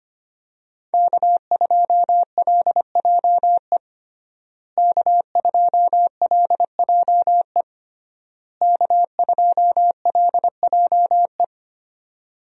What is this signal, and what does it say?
Morse code, heard as a keyed tone, about 25 wpm, 700 Hz. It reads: K2LJE K2LJE K2LJE
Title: USA Call Signs - 5 Characters - Repeated 3x - 25wpm